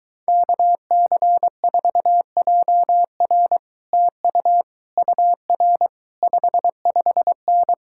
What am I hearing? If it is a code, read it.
Morse code, heard as a keyed tone, about 23 wpm, 700 Hz. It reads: KC4JR TU UR 55N